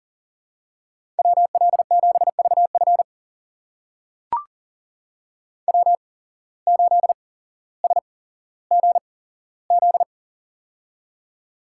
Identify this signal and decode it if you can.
Morse code, heard as a keyed tone, about 42 wpm, spaced as for 17 wpm, 700 Hz. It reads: WL7VF W8SGZ
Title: USA Call Signs - 5 Characters - 8x Character Spacing - 40wpm